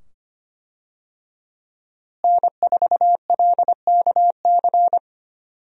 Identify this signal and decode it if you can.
Morse code, heard as a keyed tone, about 25 wpm, 700 Hz. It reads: N4LKC
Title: USA Call Signs - 5 Characters - Repeated 2x - 25wpm